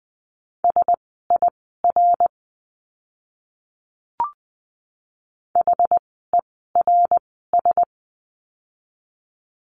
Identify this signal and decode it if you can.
Morse code, heard as a keyed tone, about 20 wpm, spaced as for 15 wpm, 700 Hz. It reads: SIR HERS
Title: 061 Lesson 11f - Letter h - Words 20wpm